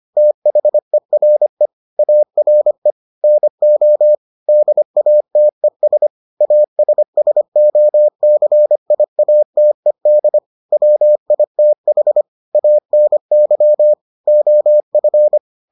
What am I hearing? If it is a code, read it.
Morse code, heard as a keyed tone, about 25 wpm, 600 Hz. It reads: THERE ARE NO DATES ASSOCIATED WITH ANY OF